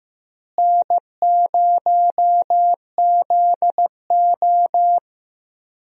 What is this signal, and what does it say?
Morse code, heard as a keyed tone, about 15 wpm, 700 Hz. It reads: N0ZO
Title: USA Call Signs - 4 Characters - Repeated 3x - 15wpm